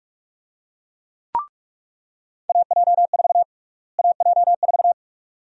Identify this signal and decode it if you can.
Morse code, heard as a keyed tone, about 44 wpm, 700 Hz. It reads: AJ4 AJ4
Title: USA Call Sign Prefixes - Repeated 2x - 45wpm